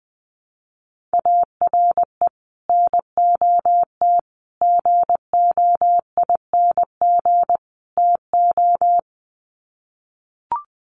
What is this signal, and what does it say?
Morse code, heard as a keyed tone, about 20 wpm, 700 Hz. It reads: ARE NOT GOING TO
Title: Sets of 4 Words 20wpm